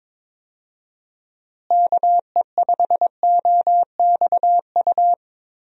Morse code, heard as a keyed tone, about 22 wpm, 700 Hz. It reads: KE5OXU
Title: USA Call Signs - 6 Characters - Repeated 3x - 22wpm